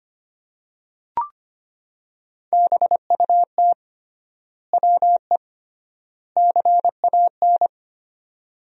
Morse code, heard as a keyed tone, about 25 wpm, 700 Hz. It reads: BUT WE CAN
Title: 3x Word-Spacing - Sentences from Top 100 Words - 25wpm